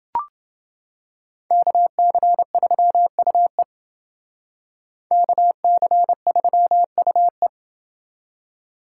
Morse code, heard as a keyed tone, about 28 wpm, 700 Hz. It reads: KC3UE KC3UE